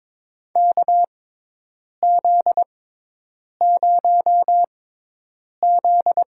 Morse code, heard as a keyed tone, about 21 wpm, spaced as for 8 wpm, 700 Hz. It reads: KZ0Z
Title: USA Call Signs - 4 Characters - 6x Character Spacing - 22wpm